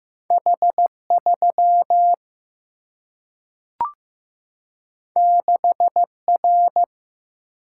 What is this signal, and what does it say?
Morse code, heard as a keyed tone, about 15 wpm, 700 Hz. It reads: H 3 6 R
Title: Single-Letter Single-Number 15wpm